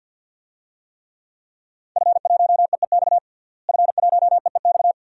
Morse code, heard as a keyed tone, about 53 wpm, 700 Hz. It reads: F1EEX F1EEX
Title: DXCC-100 Call Signs - 5 Characters - Repeated 2x - 50wpm